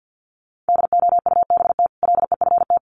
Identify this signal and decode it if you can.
Morse code, heard as a keyed tone, about 50 wpm, 700 Hz. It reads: DOUBT LEFT
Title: Sentences from Top 1500 Words 50wpm